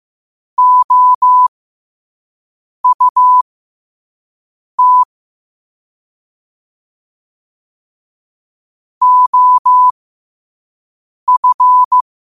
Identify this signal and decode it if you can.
Morse code, heard as a keyed tone, about 15 wpm, spaced as for 5 wpm, 1000 Hz. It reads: OUT OF